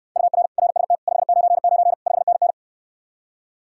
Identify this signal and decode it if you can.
Morse code, heard as a keyed tone, about 69 wpm, 700 Hz. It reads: UR RST 599 5NN